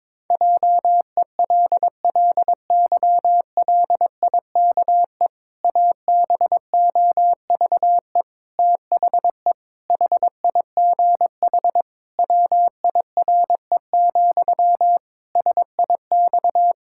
Morse code, heard as a keyed tone, about 22 wpm, 700 Hz. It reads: JELLYLIKE ABOVE THE HIGH WIRE, SIX
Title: Pangram Challenge v2 - 22wpm